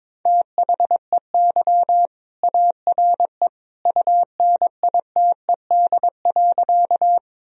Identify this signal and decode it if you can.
Morse code, heard as a keyed tone, about 22 wpm, 700 Hz. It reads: THEY ARE UNITED.
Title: Sentences from Top 1000 Words - Review Part 4 - 22wpm